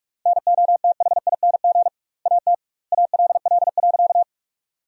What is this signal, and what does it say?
Morse code, heard as a keyed tone, about 42 wpm, 700 Hz. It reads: NOTHING AT ALL.
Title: 2x Word-Spacing - Sentences from Top 500 Words - 45wpm